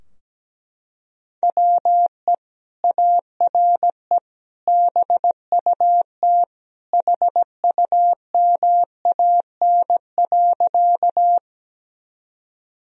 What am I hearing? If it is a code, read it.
Morse code, heard as a keyed tone, about 17 wpm, 700 Hz. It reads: WE ARE BUT HUMAN.
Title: Sentences from Top 1000 Words - Review Part 3 - 17wpm